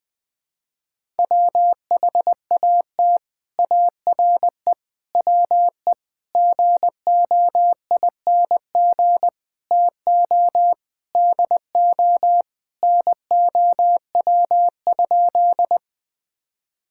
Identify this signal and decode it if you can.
Morse code, heard as a keyed tone, about 19 wpm, 700 Hz. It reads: WHAT ARE WE GOING TO DO NOW?
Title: Sentences with 6-Word Phrases - 20wpm